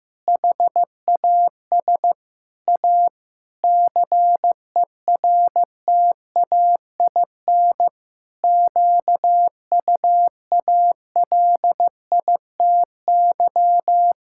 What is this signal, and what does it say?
Morse code, heard as a keyed tone, about 15 wpm, 700 Hz. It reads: HAS A CERTAIN QUALITY
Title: Sentences from Top 1000 Words - Review Part 6 - 15wpm